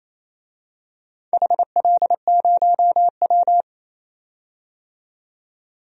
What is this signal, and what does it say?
Morse code, heard as a keyed tone, about 28 wpm, 700 Hz. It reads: HL0W
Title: DXCC-100 Call Signs - 4 Characters - 28wpm